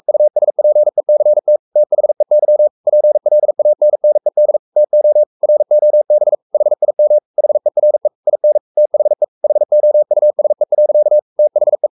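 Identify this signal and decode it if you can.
Morse code, heard as a keyed tone, about 44 wpm, 600 Hz. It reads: USPECT THEY PLANNED TO ROB HIM HERE IN THE HOUSE. THE
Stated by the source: Morse code at 43 wpm sending Ken Follett's novel Never in English, from CW med Gnister.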